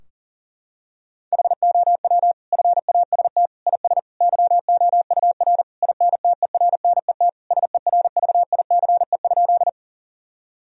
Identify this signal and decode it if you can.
Morse code, heard as a keyed tone, about 40 wpm, 700 Hz. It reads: HOW FAST IS YOUR INTERNET SERVICE?
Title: Sentences from Top 1000 Words - Review Part 4 - 40wpm